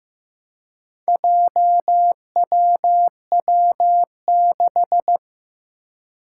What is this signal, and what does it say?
Morse code, heard as a keyed tone, about 15 wpm, 700 Hz. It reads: JWW6
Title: Four Challenging Letter-Number Combinations 15wpm